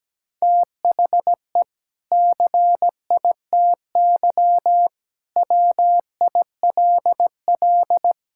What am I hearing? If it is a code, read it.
Morse code, heard as a keyed tone, about 17 wpm, 700 Hz. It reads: THE CITY WILL
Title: Sentences from Top 1600 Words 17wpm